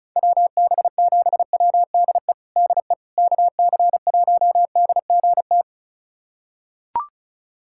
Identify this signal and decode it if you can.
Morse code, heard as a keyed tone, about 36 wpm, 700 Hz. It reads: WB7WDE DE KC1DGT
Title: USA Callsign DE Callsign - 6 Characters - 35wpm